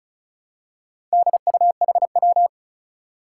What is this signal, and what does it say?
Morse code, heard as a keyed tone, about 36 wpm, 700 Hz. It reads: DUHW
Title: Four Challenging Letter-Number Combinations 35wpm